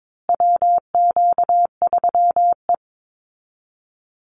Morse code, heard as a keyed tone, about 22 wpm, 700 Hz. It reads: WQ3E
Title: Call Signs Repeated 2x - 22wpm